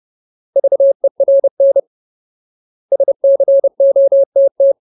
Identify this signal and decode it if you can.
Morse code, heard as a keyed tone, about 30 wpm, 550 Hz. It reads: VERN SCOTT
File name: Ultimate Word List / 30WPM Ultimate